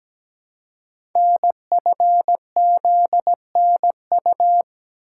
Morse code, heard as a keyed tone, about 17 wpm, 700 Hz. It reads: NFZNU
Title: Code Groups - 5 Letters - 17wpm